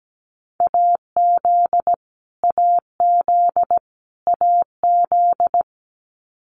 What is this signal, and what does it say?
Morse code, heard as a keyed tone, about 17 wpm, 700 Hz. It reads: AZ AZ AZ
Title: US States Repeated 3x 17wpm